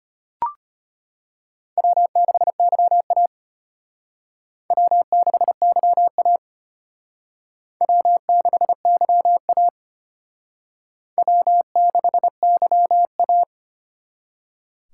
Morse code, heard as a keyed tone, about 32 wpm, 700 Hz. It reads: W6YA W6YA W6YA W6YA